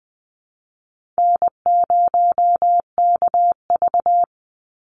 Morse code, heard as a keyed tone, about 20 wpm, 700 Hz. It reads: N0KV
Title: Call Signs Repeated 3x - 20wpm